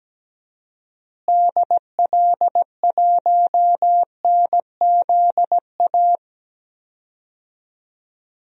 Morse code, heard as a keyed tone, about 17 wpm, 700 Hz. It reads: DL1NZA
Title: DXCC-100 Call Signs - 6 Characters - Repeated 3x - 17wpm